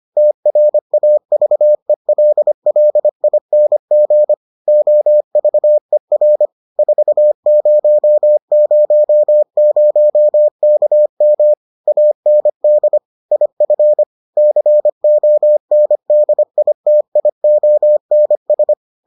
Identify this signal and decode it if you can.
Morse code, heard as a keyed tone, about 25 wpm, 600 Hz. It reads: TRAVELLING OVER 4000KM AND IF CONDITIONS